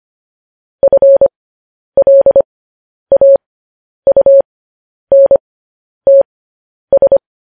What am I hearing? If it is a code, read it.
Morse code, heard as a keyed tone, about 25 wpm, spaced as for 10 wpm, 550 Hz. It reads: FLAUNTS
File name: podcast / News headlines-1x-25x10